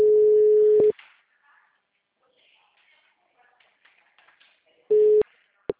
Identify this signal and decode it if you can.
Ringback tone (Europe, 425 Hz), with speech.
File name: out- -2002- - - .471